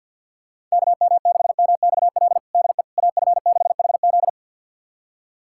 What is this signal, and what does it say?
Morse code, heard as a keyed tone, about 52 wpm, 700 Hz. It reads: KM6MXL DE AF6HZ
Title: USA Callsign DE Callsign 50wpm